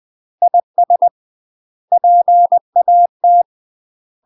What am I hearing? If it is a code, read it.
Morse code, heard as a keyed tone, about 20 wpm, 700 Hz. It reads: IS PAT
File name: UK MASTER QSOs TRAINING / 20WPM / UK MASTER QSOs 0072